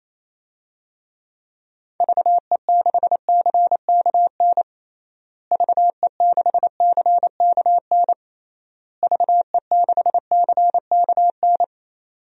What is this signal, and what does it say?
Morse code, heard as a keyed tone, about 28 wpm, 700 Hz. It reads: VE6CKN VE6CKN VE6CKN
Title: DXCC-100 Call Signs - 6 Characters - Repeated 3x - 28wpm